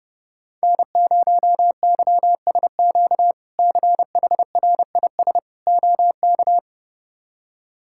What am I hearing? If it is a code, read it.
Morse code, heard as a keyed tone, about 30 wpm, 700 Hz. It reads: N0YSQ CHRIS OK
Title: K1USN SST QSO Practice - 30wpm